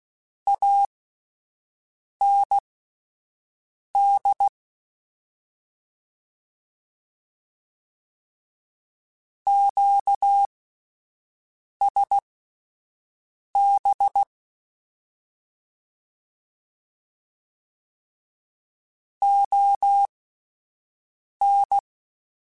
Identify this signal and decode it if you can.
Morse code, heard as a keyed tone, about 16 wpm, spaced as for 5 wpm, 800 Hz. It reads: AND QSB ON
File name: (spaced) / QSOs Spaced-mp3s / WD0BGP 1 @10 sp